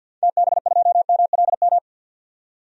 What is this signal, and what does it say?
Morse code, heard as a keyed tone, about 52 wpm, 700 Hz. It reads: TB2MLM
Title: DXCC-100 Call Signs - 6 Characters - 50wpm